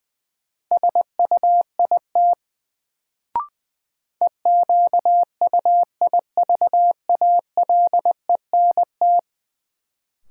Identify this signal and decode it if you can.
Morse code, heard as a keyed tone, about 20 wpm, 700 Hz. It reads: SUIT EQUIVALENT
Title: Sentences from Top 1600 Words - New Word Drill - 20wpm